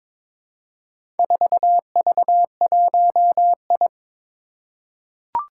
Morse code, heard as a keyed tone, about 22 wpm, 700 Hz. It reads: 4V1I
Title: Four Challenging Letter-Number Combinations 22wpm